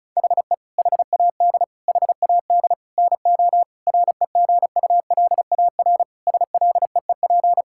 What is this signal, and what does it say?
Morse code, heard as a keyed tone, about 36 wpm, 700 Hz. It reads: HE HAD HAD NO REGULAR SLEEP
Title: Sentences from Top 1000 Words - Review Part 6 - 35wpm